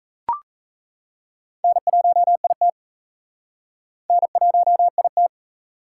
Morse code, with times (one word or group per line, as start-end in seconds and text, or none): 1.6-2.7 N1IT
4.1-5.3 N1IT